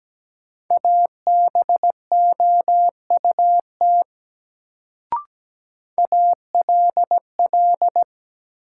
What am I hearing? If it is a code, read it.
Morse code, heard as a keyed tone, about 17 wpm, 700 Hz. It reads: ABOUT ALL
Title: Sentences from Top 100 Words - New Word Drill - 17wpm